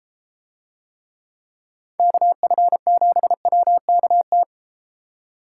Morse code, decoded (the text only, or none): KF7WKT